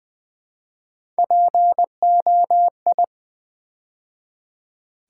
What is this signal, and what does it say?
Morse code, heard as a keyed tone, about 20 wpm, 700 Hz. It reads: POI
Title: Build A Word - 3 Letters At A Time - 6-Letter Words - 20wpm